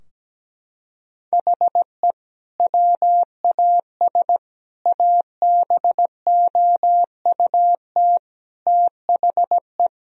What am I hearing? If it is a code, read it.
Morse code, heard as a keyed tone, about 17 wpm, 700 Hz. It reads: HE WAS ABOUT THE
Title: Sentences from Top 1000 Words - Review Part 3 - 17wpm